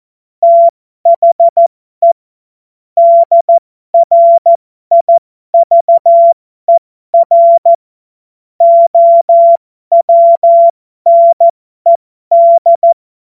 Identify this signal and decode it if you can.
Morse code, heard as a keyed tone, about 14 wpm, spaced as for 12 wpm, 680 Hz. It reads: THE DRIVER OWNED